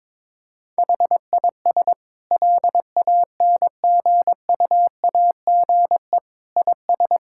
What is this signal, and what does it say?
Morse code, heard as a keyed tone, about 22 wpm, 700 Hz. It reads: HIS LANGUAGE IS
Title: Sentences from Top 1600 Words 22wpm